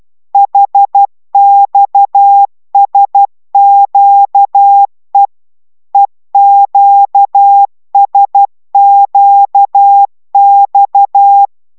Morse code, heard as a keyed tone, about 12 wpm, 800 Hz. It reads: HXSQE EQSQX
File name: taviro tanulas / cw 69 16 10